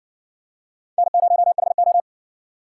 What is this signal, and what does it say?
Morse code, heard as a keyed tone, about 60 wpm, 700 Hz. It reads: N0HO